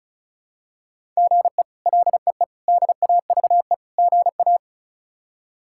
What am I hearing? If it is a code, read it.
Morse code, heard as a keyed tone, about 34 wpm, 700 Hz. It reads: GE LEE DAVE GA
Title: K1USN SST QSO Practice - 35wpm